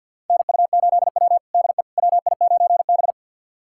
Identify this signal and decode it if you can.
Morse code, heard as a keyed tone, about 47 wpm, 700 Hz. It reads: NU8W DE WI9B
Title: USA Callsign DE Callsign - 4 Characters - 50wpm